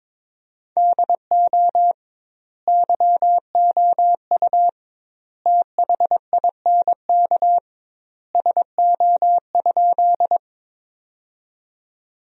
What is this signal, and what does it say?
Morse code, heard as a keyed tone, about 22 wpm, 700 Hz. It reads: DO YOU THINK SO?
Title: 2x Word-Spacing - Sentences from Top 200 Words - 22wpm